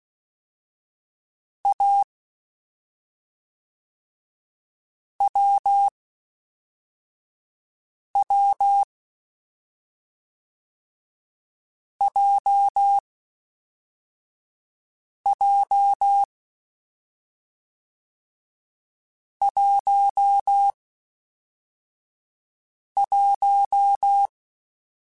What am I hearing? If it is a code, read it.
Morse code, heard as a keyed tone, about 16 wpm, 800 Hz. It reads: A W W J J 1 1
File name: Difficult Combos / Difficult Combos-mp3s / awj1 ndb6 gz7 @5